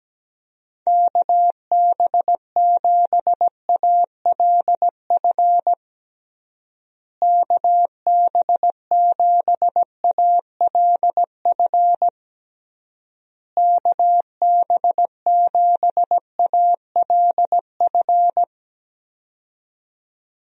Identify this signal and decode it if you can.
Morse code, heard as a keyed tone, about 17 wpm, 700 Hz. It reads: KB7ALF KB7ALF KB7ALF
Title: USA Call Signs - 6 Characters - Repeated 3x - 17wpm